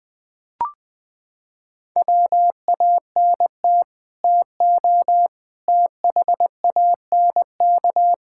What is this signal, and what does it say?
Morse code, heard as a keyed tone, about 21 wpm, 700 Hz. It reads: WANT TO THANK
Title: Sets of 9 Words - 20wpm